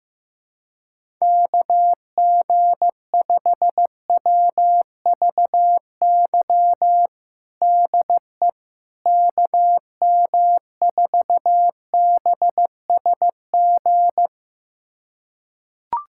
Morse code, heard as a keyed tone, about 15 wpm, 700 Hz. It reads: KG5WVY DE KM4BSG
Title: USA Callsign DE Callsign - 6 Characters - 15wpm